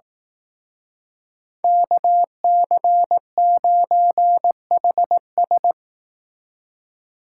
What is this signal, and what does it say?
Morse code, heard as a keyed tone, about 18 wpm, 700 Hz. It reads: KC9HS
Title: Speed Racing - USA 2x2 Call Signs 15wpm